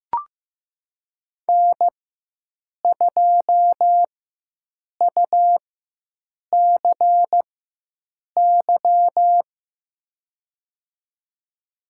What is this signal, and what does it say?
Morse code, heard as a keyed tone, about 15 wpm, spaced as for 7 wpm, 700 Hz. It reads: N2UCY
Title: USA Call Signs - 5 Characters - 4x Character Spacing - 15wpm